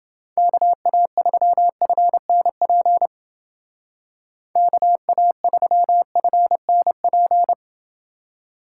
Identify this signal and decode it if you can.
Morse code, heard as a keyed tone, about 28 wpm, 700 Hz. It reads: KA3FNP KA3FNP